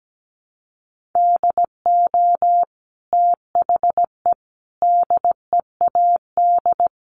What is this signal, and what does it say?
Morse code, heard as a keyed tone, about 17 wpm, 700 Hz. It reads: DO THE DEAD
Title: Sentences from Top 1500 Words 17wpm